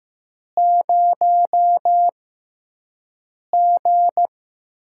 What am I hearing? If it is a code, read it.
Morse code, heard as a keyed tone, about 15 wpm, spaced as for 5 wpm, 700 Hz. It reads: 0G
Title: USA Call Signs - 6 Characters - 6x Character Spacing - 15wpm